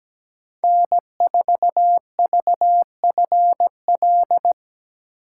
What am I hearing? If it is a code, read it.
Morse code, heard as a keyed tone, about 17 wpm, 700 Hz. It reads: N4VFL
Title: USA Call Signs - 3x Character Spacing - 17wpm